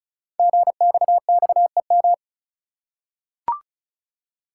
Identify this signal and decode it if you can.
Morse code, heard as a keyed tone, about 36 wpm, 690 Hz. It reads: GXXEM
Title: Code Groups - 5 Letters - 35wpm